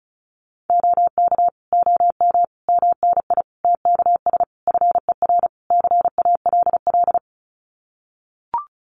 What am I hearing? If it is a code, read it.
Morse code, heard as a keyed tone, about 34 wpm, 700 Hz. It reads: OK OM MNI TKS FER CALL
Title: Common QSO Phrases 35wpm